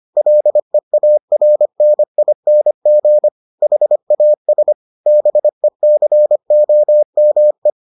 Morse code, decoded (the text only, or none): LEARNING HAS BECOME